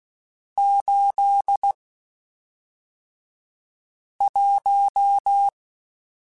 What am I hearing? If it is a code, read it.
Morse code, heard as a keyed tone, about 16 wpm, spaced as for 3 wpm, 800 Hz. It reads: 81
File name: Spaced Characters / Spaced Characters-mp3s / Weather5 12 sp @5